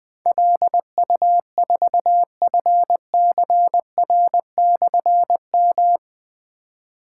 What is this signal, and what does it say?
Morse code, heard as a keyed tone, about 20 wpm, 700 Hz. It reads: LU4FCR/M
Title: Difficult Call Signs 20wpm